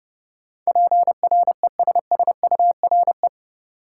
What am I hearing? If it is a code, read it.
Morse code, heard as a keyed tone, about 30 wpm, 700 Hz. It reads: PRESSURE
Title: Sentences from Top 1000 Words - Review Part 6 - New Word Drill - 30wpm